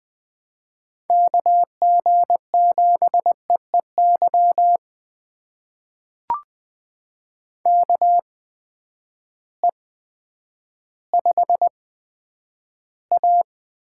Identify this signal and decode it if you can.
Morse code, heard as a keyed tone, about 20 wpm, spaced as for 10 wpm, 700 Hz. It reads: KG7EEY KE5A
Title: USA Call Signs - 6 Characters - 8x Character Spacing - 20wpm